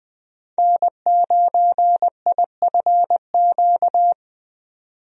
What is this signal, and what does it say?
Morse code, heard as a keyed tone, about 21 wpm, 700 Hz. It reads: N9IFQ
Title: USA Call Signs - 5 Characters - 20wpm